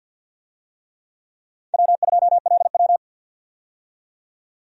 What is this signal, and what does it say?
Morse code, heard as a keyed tone, about 53 wpm, 700 Hz. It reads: W2LW
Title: USA Call Signs - 4 Characters - 50wpm